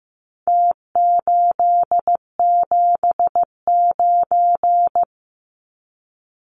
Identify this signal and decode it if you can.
Morse code, heard as a keyed tone, about 15 wpm, 700 Hz. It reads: T879
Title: Numbers - Contest Style - 4-Digit 15wpm